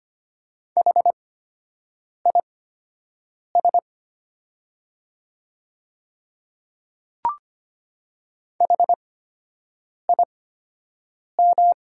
Morse code, heard as a keyed tone, about 25 wpm, spaced as for 7 wpm, 700 Hz. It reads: HIS HIM
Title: 8x Character Spacing - Top 500 Words - 25wpm